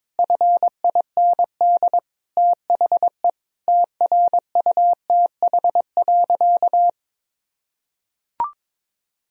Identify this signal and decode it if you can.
Morse code, heard as a keyed tone, about 22 wpm, 700 Hz. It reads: FIND THE TRUTH.
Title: Sentences from Top 1000 Words - Review Part 3 - 22wpm